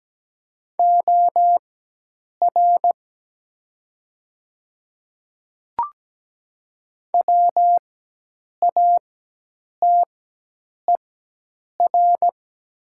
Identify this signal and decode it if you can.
Morse code, heard as a keyed tone, about 17 wpm, spaced as for 8 wpm, 700 Hz. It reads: OR WATER
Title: 4x Character Spacing - Top 100 Words - 17wpm